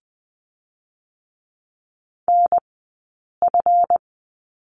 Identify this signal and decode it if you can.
Morse code, heard as a keyed tone, about 20 wpm, spaced as for 8 wpm, 700 Hz. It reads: NF
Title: 137 Lesson 23d - Letter f - Two Characters 20wpm